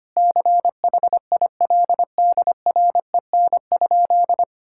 Morse code, heard as a keyed tone, about 25 wpm, 700 Hz. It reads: CHILDREN?
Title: Sentences from Top 1000 Words - Review Part 5 - 25wpm